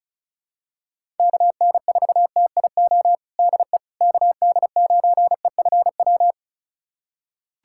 Morse code, heard as a keyed tone, about 36 wpm, 700 Hz. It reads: KN4TIO DE KD9EFW
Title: USA Callsign DE Callsign - 6 Characters - 35wpm